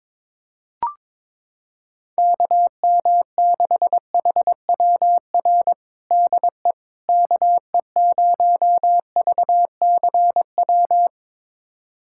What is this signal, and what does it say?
Morse code, heard as a keyed tone, about 22 wpm, 700 Hz. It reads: KM6HWR DE KE0VCW
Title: USA Callsign DE Callsign - 6 Characters - 22wpm